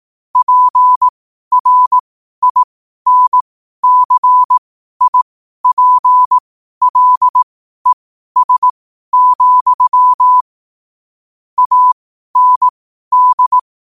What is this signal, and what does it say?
Morse code, heard as a keyed tone, about 18 wpm, spaced as for 12 wpm, 1000 Hz. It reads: PRINCIPLES, AND
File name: qotdfiles / 12wpm